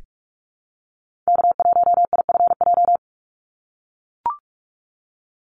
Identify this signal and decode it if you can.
Morse code, heard as a keyed tone, about 44 wpm, 700 Hz. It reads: K1IFJ
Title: Call Signs - 45wpm